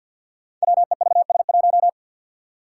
Morse code, heard as a keyed tone, about 51 wpm, 700 Hz. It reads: WEVS1
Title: Five Challenging Letter-Number Combinations 50wpm